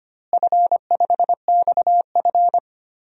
Morse code, heard as a keyed tone, about 25 wpm, 700 Hz. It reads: F5XF
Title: DXCC-100 Call Signs - 4 Characters - Repeated 3x - 25wpm